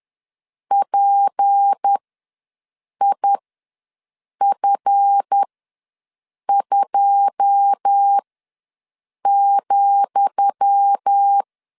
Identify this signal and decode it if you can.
Morse code, heard as a keyed tone, about 10 wpm, spaced as for 6 wpm, 800 Hz. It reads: PIF2,
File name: LEZIONI / 10 SIGNS / 10D COMA ws 10wpm 4314